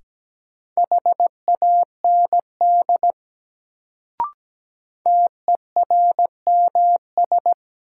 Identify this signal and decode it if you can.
Morse code, heard as a keyed tone, about 17 wpm, 700 Hz. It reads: HAND TERMS
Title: Sets of 3 Words - New Word Drill - 17wpm